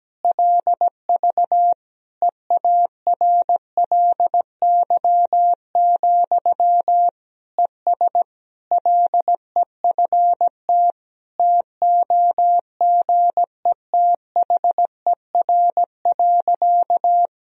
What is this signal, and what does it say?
Morse code, heard as a keyed tone, about 17 wpm, 700 Hz. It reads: LV EARLY, ES LEFT TOGETHER.
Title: Sentences from Top 400 Words and Abbreviations 17wpm